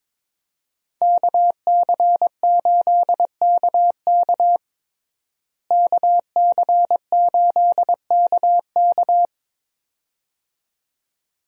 Morse code, heard as a keyed tone, about 22 wpm, 700 Hz. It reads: KC8KK KC8KK
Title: USA Call Signs - 5 Characters - Repeated 2x - 22wpm